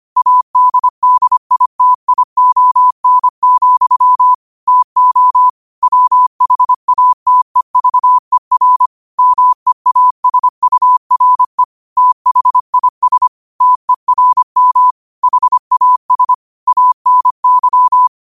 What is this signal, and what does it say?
Morse code, heard as a keyed tone, about 25 wpm, 1000 Hz. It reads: ADDITION, TO WHATEVER MEASURE THIS TERM HAS ANY